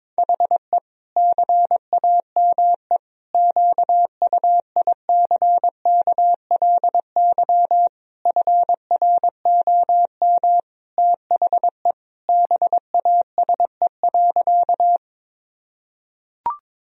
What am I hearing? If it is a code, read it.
Morse code, heard as a keyed tone, about 22 wpm, 700 Hz. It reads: HE CAME QUICKLY FROM THE BASE.
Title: Sentences from Top 1000 Words - Review Part 1 - 22wpm